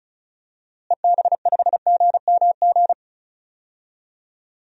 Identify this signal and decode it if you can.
Morse code, heard as a keyed tone, about 36 wpm, 700 Hz. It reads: EB5GMG
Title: DXCC-100 Call Signs - 6 Characters - Repeated 3x - 35wpm